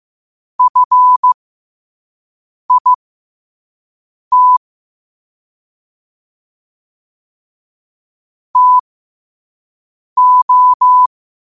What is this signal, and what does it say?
Morse code, heard as a keyed tone, about 15 wpm, spaced as for 5 wpm, 1000 Hz. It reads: FIT TO
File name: qotdfiles / 5wpm